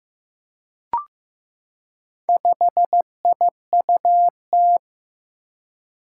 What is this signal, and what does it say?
Morse code, heard as a keyed tone, about 15 wpm, 700 Hz. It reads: 5IUT
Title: Four Challenging Letter-Number Combinations 15wpm